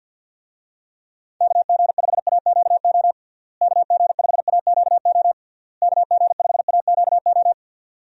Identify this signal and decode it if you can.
Morse code, heard as a keyed tone, about 52 wpm, 700 Hz. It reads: KG5AQO KG5AQO KG5AQO
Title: USA Call Signs - 6 Characters - Repeated 3x - 50wpm